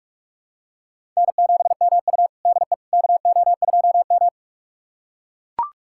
Morse code, heard as a keyed tone, about 41 wpm, 700 Hz. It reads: N7MU DE KO2M
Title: USA Callsign DE Callsign - 4 Characters - 45wpm